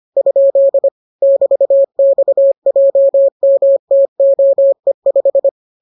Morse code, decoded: ? <BT>XJMTOE5

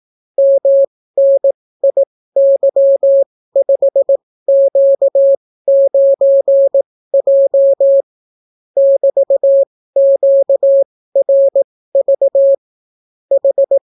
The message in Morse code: MNIY5Q9J <BT>QRV H

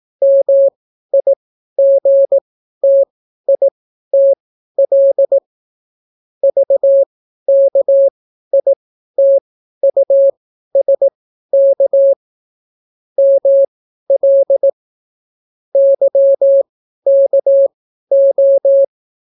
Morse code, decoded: MIGTITL VKITUSK ML YKO

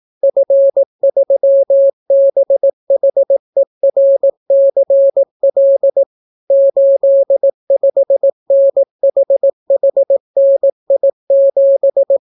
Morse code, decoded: F3BHERCL 85NHHNI7

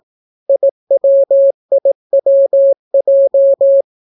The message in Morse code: IWIWJ

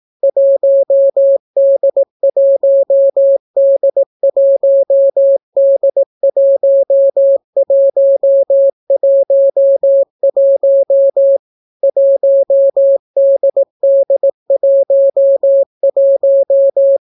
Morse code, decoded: 1D1D1D1111 1DD11

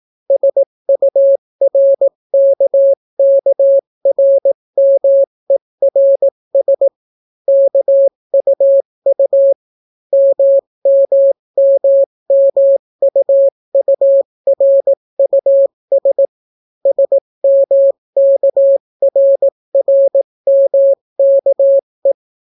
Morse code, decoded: SURKKRMERS KUU MMMMUURUS SMKRRMKE

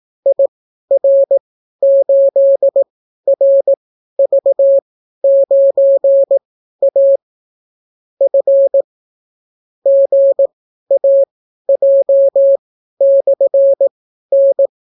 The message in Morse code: IR8RV9A F GAJ/N

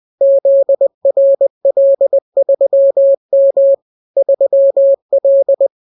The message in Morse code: ZRL3M 3L

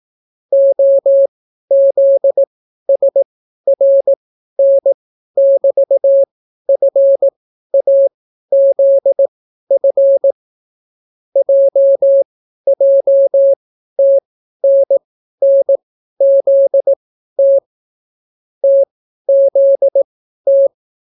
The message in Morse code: OZSRN<BT>FAZF JJTNNZT TZT

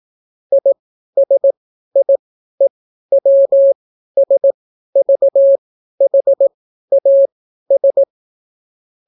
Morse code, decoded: ISIEWSVHAS

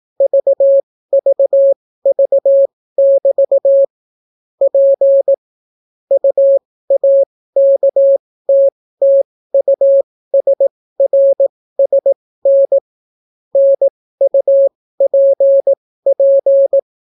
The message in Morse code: VVV<BT> P UAKTTUSRSN NUPP

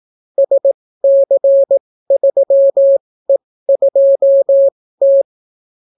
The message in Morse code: SC3E2T

